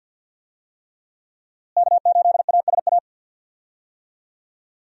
Morse code, heard as a keyed tone, about 51 wpm, 700 Hz. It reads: K8ASA
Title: USA Call Signs - 5 Characters - Repeated 3x - 50wpm